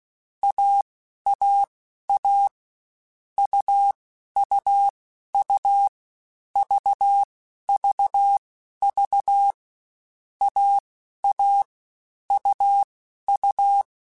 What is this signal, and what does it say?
Morse code, heard as a keyed tone, about 16 wpm, spaced as for 11 wpm, 800 Hz. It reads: AAA UUU VVV AA UU
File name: Difficult Combos / Difficult Combos-mp3s / AUV ShortRun @15